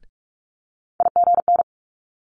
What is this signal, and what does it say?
Morse code, heard as a keyed tone, about 46 wpm, 700 Hz. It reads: IGN
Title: Multi-letter Phonemes Quick Fire 45wpm